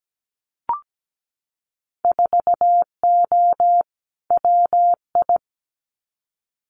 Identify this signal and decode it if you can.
Morse code, heard as a keyed tone, about 17 wpm, 700 Hz. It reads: 4O WI
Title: Winter Field Day Exchanges 17wpm